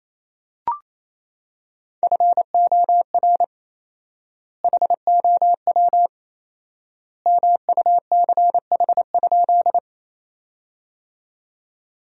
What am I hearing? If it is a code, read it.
Morse code, heard as a keyed tone, about 28 wpm, 700 Hz. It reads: FOR HOW MUCH?
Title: 4x Word-Spacing - Sentences from Top 200 Words - 28wpm